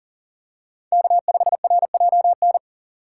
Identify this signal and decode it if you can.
Morse code, heard as a keyed tone, about 38 wpm, 700 Hz. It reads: K5RJN
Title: USA Call Signs - 5 Characters - 2x Character Spacing - 40wpm